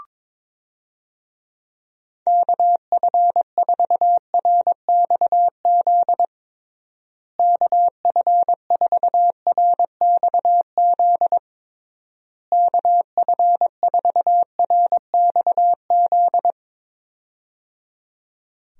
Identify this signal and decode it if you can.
Morse code, heard as a keyed tone, about 22 wpm, 700 Hz. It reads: KF4RXZ KF4RXZ KF4RXZ
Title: USA Call Signs - 6 Characters - Repeated 3x - 22wpm